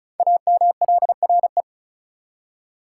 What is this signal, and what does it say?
Morse code, heard as a keyed tone, about 36 wpm, 700 Hz. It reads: AMLRE
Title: Code Groups - 5 Letters - 35wpm